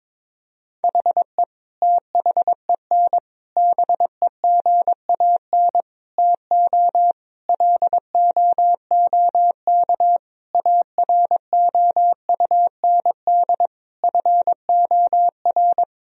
Morse code, heard as a keyed tone, about 22 wpm, 700 Hz. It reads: HE THEN BEGAN TO LOOK AROUND FOR
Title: Sentences from Top 1000 Words - Review Part 3 - 22wpm